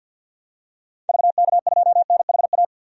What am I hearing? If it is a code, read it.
Morse code, heard as a keyed tone, about 50 wpm, 700 Hz. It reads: VK2NHA
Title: DXCC-100 Call Signs - 6 Characters - 50wpm